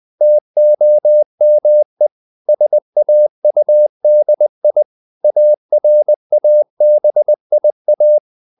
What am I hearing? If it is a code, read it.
Morse code, heard as a keyed tone, about 20 wpm, 600 Hz. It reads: TOME SAUDI ARABIA
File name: CW / Practice / NationalCapitals20